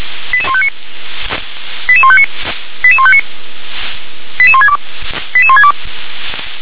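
Five-tone selective call (ZVEI), heard as quick runs of five tones, tones of about 70 ms. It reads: 88159 88159 88159 88162 88162
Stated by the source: Five-tone ZVEI selective call received on an SDR.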